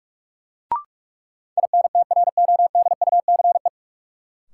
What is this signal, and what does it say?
Morse code, heard as a keyed tone, about 47 wpm, 700 Hz. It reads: INTRODUCE
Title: Sentences from Top 1600 Words - New Word Drill - 45wpm